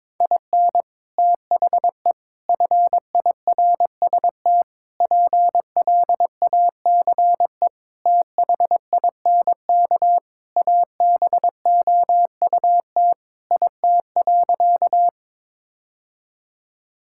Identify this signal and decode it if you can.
Morse code, heard as a keyed tone, about 22 wpm, 700 Hz. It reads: IN THE FIRST PLACE THINK ABOUT IT.
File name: practice / 22wpm / Sentences with 3-Word Phrases - 22wpm